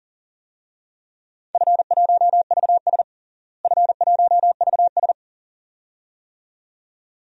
Morse code, heard as a keyed tone, about 40 wpm, 700 Hz. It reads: F1VS F1VS
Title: DXCC-100 Call Signs - 4 Characters - Repeated 2x - 40wpm